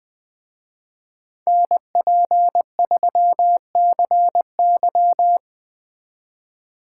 Morse code, heard as a keyed tone, about 20 wpm, 700 Hz. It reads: NP3CY
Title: Speed Racing - USA 2x2 Call Signs 17wpm